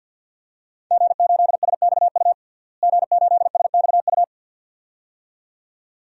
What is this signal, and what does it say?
Morse code, heard as a keyed tone, about 52 wpm, 700 Hz. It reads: G8SXU G8SXU
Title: DXCC-100 Call Signs - 5 Characters - Repeated 2x - 50wpm